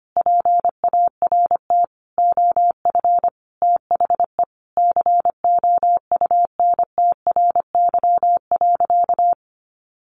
Morse code, heard as a keyed tone, about 25 wpm, 700 Hz. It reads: PART OF THE COUNTRY.
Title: Sentences from Top 300 Words 25wpm